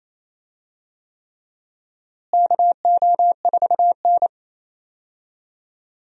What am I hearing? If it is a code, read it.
Morse code, heard as a keyed tone, about 28 wpm, 700 Hz. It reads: KO4N
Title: USA Call Signs - 4 Characters - Repeated 3x - 28wpm